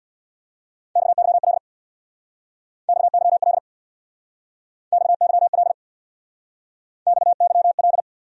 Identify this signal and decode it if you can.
Morse code, heard as a keyed tone, about 63 wpm, 700 Hz. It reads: XYL XYL XYL XYL